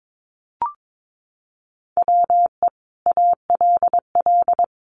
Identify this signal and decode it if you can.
Morse code, heard as a keyed tone, about 22 wpm, 700 Hz. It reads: WE ALL
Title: Sentences from Top 1200 Words 22wpm